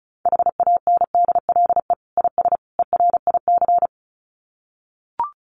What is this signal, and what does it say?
Morse code, heard as a keyed tone, about 34 wpm, 700 Hz. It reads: HANDLE IS ERIC
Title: Common QSO Phrases 35wpm